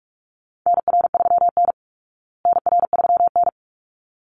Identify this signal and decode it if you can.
Morse code, heard as a keyed tone, about 44 wpm, 700 Hz. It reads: NR3N NR3N
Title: Call Signs Repeated 3x - 45wpm